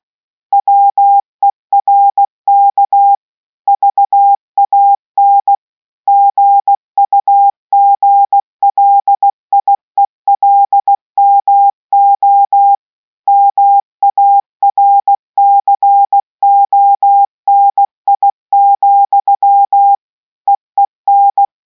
Morse code, heard as a keyed tone, about 16 wpm, 800 Hz. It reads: WERK VAN GUGLIELMO MARCONI, EEN